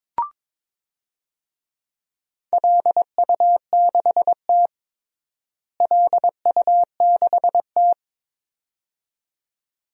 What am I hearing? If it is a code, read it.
Morse code, heard as a keyed tone, about 22 wpm, 700 Hz. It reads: LU6T LU6T
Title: DXCC-100 Call Signs - 4 Characters - Repeated 2x - 22wpm